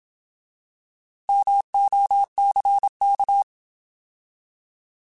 Morse code, heard as a keyed tone, about 27 wpm, 800 Hz. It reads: MOCK
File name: Alphabetical / Alphabetical mp3s / Ms @25